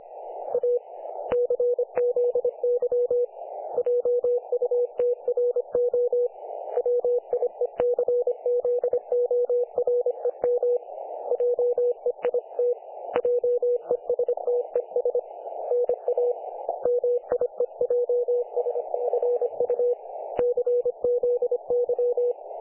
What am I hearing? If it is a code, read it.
Morse code, heard as a keyed tone, about 26 wpm, 500 Hz. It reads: A CZY JUTRO WIECZOREM JEIT JESTES NA MIEJSFU CZY
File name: dlazaawansowanych / nagraniazpasma / websdr recording 3564.7kHz